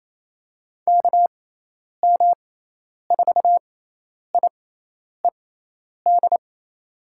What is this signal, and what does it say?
Morse code, heard as a keyed tone, about 28 wpm, spaced as for 10 wpm, 700 Hz. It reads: KM4IED
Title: USA Call Signs - 6 Characters - 6x Character Spacing - 28wpm